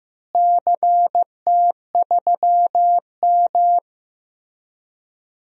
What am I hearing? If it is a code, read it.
Morse code, heard as a keyed tone, about 15 wpm, 700 Hz. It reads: CT3M
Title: DXCC-100 Call Signs - 4 Characters - Repeated 2x - 15wpm